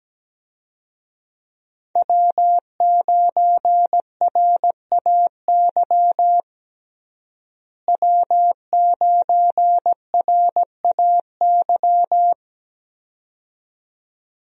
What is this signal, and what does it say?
Morse code, heard as a keyed tone, about 17 wpm, 700 Hz. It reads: W9RAY W9RAY
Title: USA Call Signs - 5 Characters - Repeated 2x - 17wpm